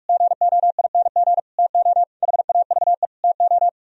Morse code, heard as a keyed tone, about 47 wpm, 700 Hz. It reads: GOING TO HAVE TO